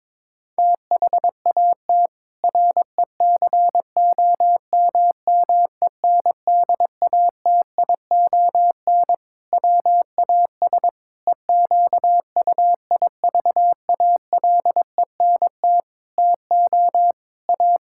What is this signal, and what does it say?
Morse code, heard as a keyed tone, about 22 wpm, 700 Hz. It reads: THAT RECOMMENDATION WAS EQUIVALENT TO A
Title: Sentences from Top 1600 Words 22wpm